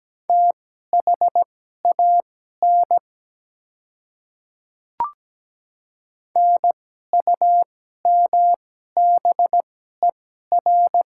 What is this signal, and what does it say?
Morse code, heard as a keyed tone, about 17 wpm, spaced as for 12 wpm, 700 Hz. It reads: THAN NUMBER
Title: 2x Character Spacing - Top 100 Words - 17wpm